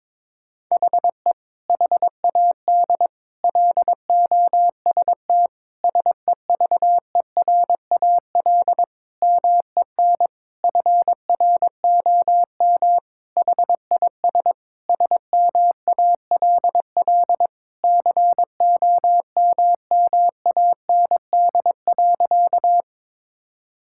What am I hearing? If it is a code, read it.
Morse code, heard as a keyed tone, about 22 wpm, 700 Hz. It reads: HE HAD LOST SEVERAL MEN FROM HIS SMALL COMMAND.
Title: Sentences from Top 1600 Words 22wpm